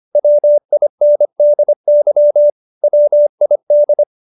Morse code, heard as a keyed tone, about 25 wpm, 590 Hz. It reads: WINDY WID